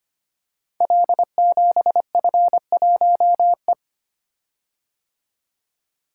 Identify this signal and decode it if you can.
Morse code, heard as a keyed tone, about 25 wpm, 700 Hz. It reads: L7F1E
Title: Code Groups - 5 Letter:Numbers - 25wpm